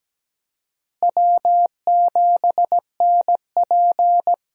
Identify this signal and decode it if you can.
Morse code, heard as a keyed tone, about 17 wpm, 700 Hz. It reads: W7NP
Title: USA Call Signs - 4 Characters - Repeated 3x - 17wpm